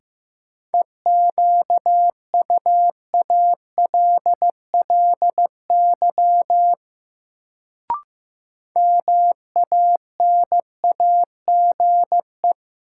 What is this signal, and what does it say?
Morse code, heard as a keyed tone, about 15 wpm, 700 Hz. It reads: EQUALLY MANAGE
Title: Sentences from Top 1400 Words - New Word Drill Alt - 15wpm